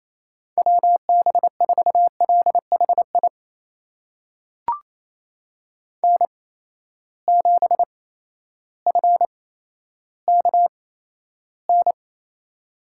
Morse code, heard as a keyed tone, about 28 wpm, spaced as for 13 wpm, 700 Hz. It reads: WB4LHI N7FKN
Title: USA Call Signs - 8x Character Spacing - 28wpm